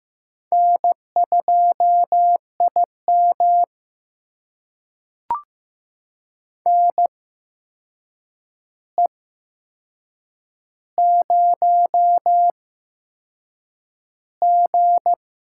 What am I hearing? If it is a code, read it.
Morse code, heard as a keyed tone, about 15 wpm, spaced as for 6 wpm, 700 Hz. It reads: N2IM NE0G
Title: USA Call Signs - 4 Characters - 8x Character Spacing - 15wpm